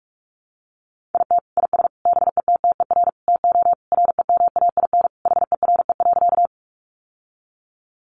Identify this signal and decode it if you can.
Morse code, heard as a keyed tone, about 44 wpm, 700 Hz. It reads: IT IS BETTER TO REMAIN HERE.
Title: Sentences from Top 1200 Words 45wpm